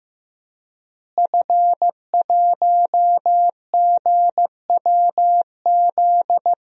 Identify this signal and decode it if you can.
Morse code, heard as a keyed tone, about 15 wpm, 700 Hz. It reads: F1GWZ
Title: DXCC-100 Call Signs - 5 Characters - Repeated 2x - 15wpm